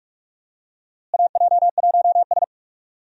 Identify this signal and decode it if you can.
Morse code, heard as a keyed tone, about 41 wpm, 700 Hz. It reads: AJ1S
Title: USA Call Signs - 4 Characters - Repeated 2x - 45wpm